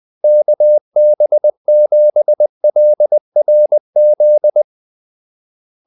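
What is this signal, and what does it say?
Morse code, heard as a keyed tone, about 20 wpm, 600 Hz. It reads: KB7LRZ
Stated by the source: Morse call-sign practice at 20 wpm.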